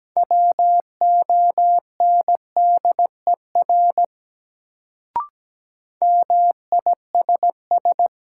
Morse code, heard as a keyed tone, about 17 wpm, 700 Hz. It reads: WONDER MISS
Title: Sentences from Top 500 Words - New Word Drill - 17wpm